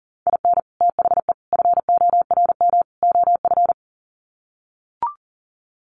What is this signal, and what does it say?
Morse code, heard as a keyed tone, about 40 wpm, 700 Hz. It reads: IN THE FORM OF
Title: Sets of 4 Words 40wpm